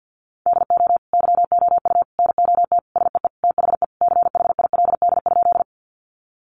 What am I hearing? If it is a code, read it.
Morse code, heard as a keyed tone, about 50 wpm, 700 Hz. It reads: DO YOU NOT SEE THE CHILD?
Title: Sentences from Top 800 Words 50wpm